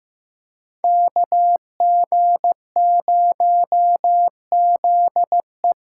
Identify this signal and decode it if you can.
Morse code, heard as a keyed tone, about 15 wpm, 700 Hz. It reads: KG0ZE
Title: USA Callsign DE Callsign - 5 Characters - 15wpm